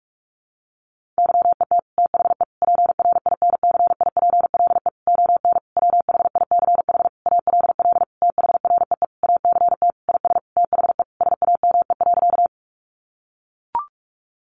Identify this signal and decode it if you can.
Morse code, heard as a keyed tone, about 44 wpm, 700 Hz. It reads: YET THE PRINCIPLE ON WHICH ALL THREE ACT IS THE SAME.
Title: Sentences from Top 1400 Words 45wpm